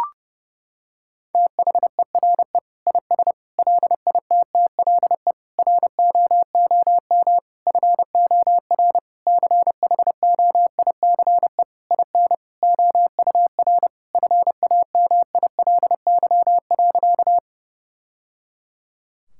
Morse code, heard as a keyed tone, about 30 wpm, 700 Hz. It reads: THERE IS LITTLE ROOM FOR CHOICE IN OUR FAMILY.
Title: Sentences from Top 1000 Words - Review Part 6 - 30wpm